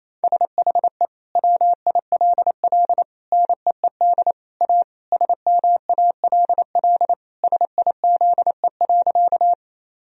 Morse code, heard as a keyed tone, about 28 wpm, 700 Hz. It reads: SHE WILL NEED A SMALL SIZE.
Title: Sentences from Top 1000 Words - Review Part 2 - 28wpm